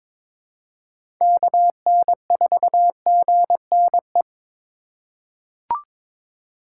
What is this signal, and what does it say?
Morse code, heard as a keyed tone, about 22 wpm, 700 Hz. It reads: KN4GNE
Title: USA Call Signs - 6 Characters - 2x Character Spacing - 22wpm